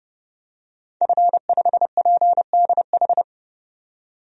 Morse code, heard as a keyed tone, about 30 wpm, 700 Hz. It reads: F5PDH
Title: DXCC-100 Call Signs - 5 Characters - 30wpm